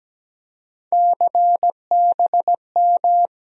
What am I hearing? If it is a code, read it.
Morse code, heard as a keyed tone, about 17 wpm, 700 Hz. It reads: CBM
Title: USA Call Sign Postfixes - 3 Letters 17wpm